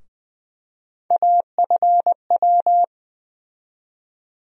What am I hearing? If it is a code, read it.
Morse code, heard as a keyed tone, about 20 wpm, 700 Hz. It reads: AFW
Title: USA Call Sign Postfixes - 3 Letters 20wpm